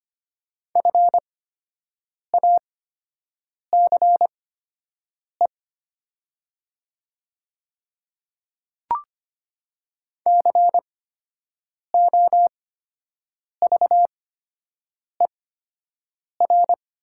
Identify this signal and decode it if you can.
Morse code, heard as a keyed tone, about 25 wpm, spaced as for 7 wpm, 700 Hz. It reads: FACE COVER
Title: 8x Character Spacing - Top 500 Words - 25wpm